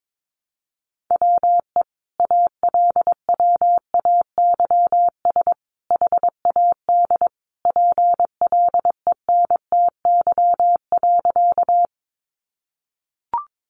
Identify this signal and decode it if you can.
Morse code, heard as a keyed tone, about 22 wpm, 700 Hz. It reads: WE ALWAYS HAD PLENTY.
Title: Sentences from Top 1300 Words 22wpm